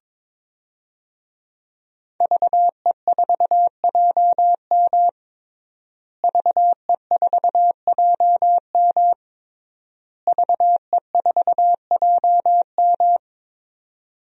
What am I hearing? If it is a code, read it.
Morse code, heard as a keyed tone, about 22 wpm, 700 Hz. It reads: VE4JM VE4JM VE4JM
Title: DXCC-100 Call Signs - 5 Characters - Repeated 3x - 22wpm